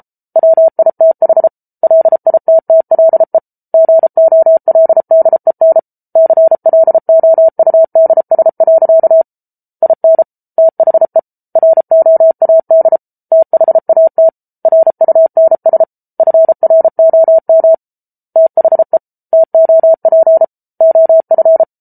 Morse code, heard as a keyed tone, about 34 wpm, 650 Hz. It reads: WITH LITTLE GOLDEN CLOUDS. IN THE ROAD THAT RUNS FROM THE TOP OF